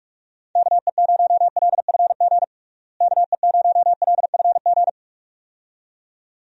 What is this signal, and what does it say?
Morse code, heard as a keyed tone, about 45 wpm, 700 Hz. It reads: KE0LFG KE0LFG
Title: USA Call Signs - 6 Characters - Repeated 2x - 45wpm